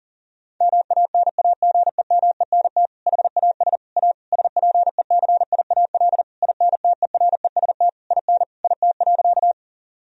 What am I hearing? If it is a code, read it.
Morse code, heard as a keyed tone, about 37 wpm, 700 Hz. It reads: MANAGEMENT HAS A SPECIAL INTEREST IN IT.
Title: Sentences from Top 1000 Words - Review Part 4 - 40wpm